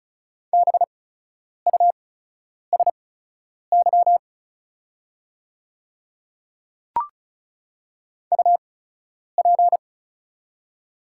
Morse code, heard as a keyed tone, about 36 wpm, spaced as for 10 wpm, 690 Hz. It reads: BUSY UP
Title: 8x Character Spacing - Top 500 Words - 35wpm